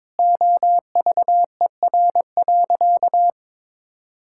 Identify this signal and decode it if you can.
Morse code, heard as a keyed tone, about 22 wpm, 700 Hz. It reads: OVER.
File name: practice / 22wpm / Sentences with 3-Word Phrases - 22wpm